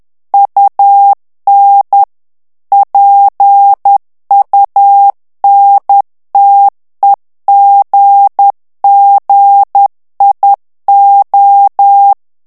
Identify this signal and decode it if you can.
Morse code, heard as a keyed tone, about 10 wpm, 800 Hz. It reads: UN PUNTEGGIO
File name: ITALIEN texte clair 2 vitesse 10 mots minute